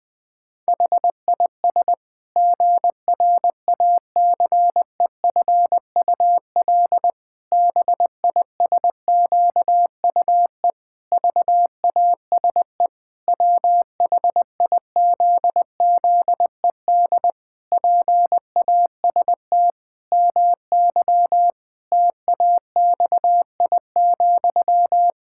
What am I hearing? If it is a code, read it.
Morse code, heard as a keyed tone, about 19 wpm, 700 Hz. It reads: HIS GRACEFUL BISQUE VASE WHIZZED PAST MY TAXI,
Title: Pangram Challenge v2 - 20wpm